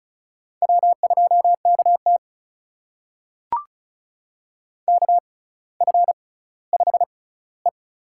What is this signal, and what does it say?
Morse code, heard as a keyed tone, about 36 wpm, spaced as for 18 wpm, 700 Hz. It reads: W2KT KF5E
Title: USA Call Signs - 4 Characters - 6x Character Spacing - 35wpm